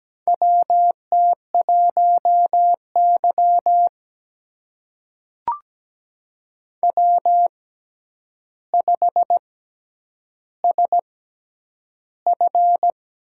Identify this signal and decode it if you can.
Morse code, heard as a keyed tone, about 17 wpm, spaced as for 9 wpm, 700 Hz. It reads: WT1Y W5SF
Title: USA Call Signs - 6x Character Spacing - 17wpm